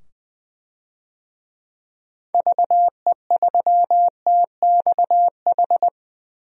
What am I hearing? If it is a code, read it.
Morse code, heard as a keyed tone, about 20 wpm, 700 Hz. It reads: VE3TXH
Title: DXCC-100 Call Signs - 6 Characters - 20wpm